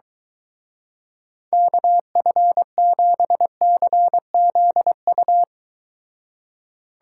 Morse code, heard as a keyed tone, about 23 wpm, 700 Hz. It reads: KF7CZU